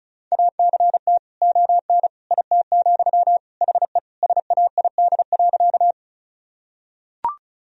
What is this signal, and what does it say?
Morse code, heard as a keyed tone, about 36 wpm, 700 Hz. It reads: ACT ON IT, HE SAID.
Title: Sentences from Top 1000 Words - Review Part 1 - 35wpm